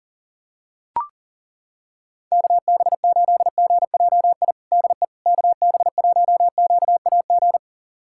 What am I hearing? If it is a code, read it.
Morse code, heard as a keyed tone, about 40 wpm, 700 Hz. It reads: KB8GJI DE KB1QAG
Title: USA Callsign DE Callsign 40wpm